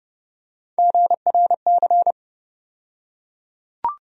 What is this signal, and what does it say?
Morse code, heard as a keyed tone, about 30 wpm, 700 Hz. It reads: GRC
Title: USA Call Sign Postfixes - 3 Letters 30wpm